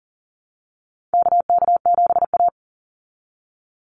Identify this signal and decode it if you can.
Morse code, heard as a keyed tone, about 40 wpm, 700 Hz. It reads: KK7A
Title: Call Signs Repeated 2x - 40wpm